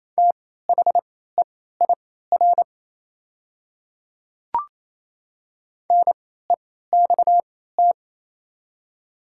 Morse code, heard as a keyed tone, about 28 wpm, spaced as for 16 wpm, 700 Hz. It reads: THEIR NEXT
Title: 3x Character Spacing - Top 300 Words - 28wpm